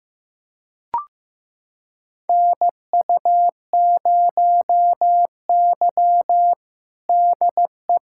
Morse code, heard as a keyed tone, about 15 wpm, 700 Hz. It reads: NU0Y DE
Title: USA Callsign DE Callsign - 4 Characters - 15wpm